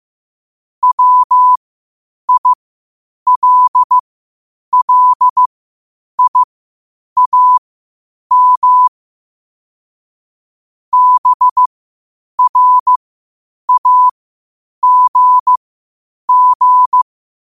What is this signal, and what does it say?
Morse code, heard as a keyed tone, about 15 wpm, spaced as for 8 wpm, 1000 Hz. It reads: WILLIAM BRAGG
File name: qotdfiles / 8wpm